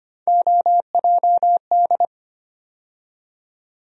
Morse code, heard as a keyed tone, about 25 wpm, 700 Hz. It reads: OJD
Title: USA Call Sign Postfixes - 3 Letters 25wpm